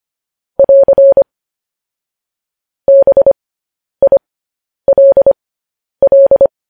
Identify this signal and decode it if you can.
Morse code, heard as a keyed tone, about 25 wpm, spaced as for 10 wpm, 550 Hz. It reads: <AR> BILL
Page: News Headlines in Morse Code at 10 WPM